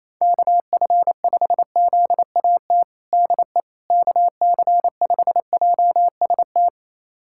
Morse code, heard as a keyed tone, about 28 wpm, 700 Hz. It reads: KF5ZAT DE KC5JST
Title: USA Callsign DE Callsign - 6 Characters - 28wpm